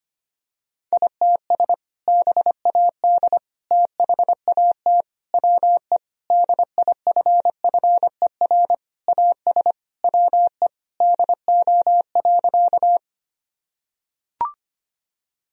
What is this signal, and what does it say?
Morse code, heard as a keyed tone, about 25 wpm, 700 Hz. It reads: ITS BAD THAT WE DIFFER AS WE DO.
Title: Sentences from Top 1000 Words - Review Part 1 - 25wpm